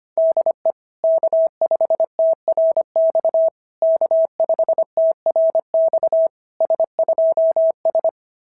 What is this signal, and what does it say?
Morse code, heard as a keyed tone, about 25 wpm, 660 Hz. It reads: DE K5TRX K5TRX S2S
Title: SOTA QSO Practice 25wpm